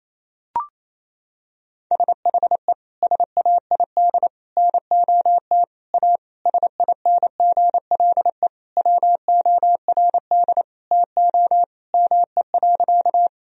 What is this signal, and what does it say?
Morse code, heard as a keyed tone, about 28 wpm, 700 Hz. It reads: SHE SAID NOT A SINGLE WORD TO ME.
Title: Sentences from Top 1000 Words - Review Part 4 - 28wpm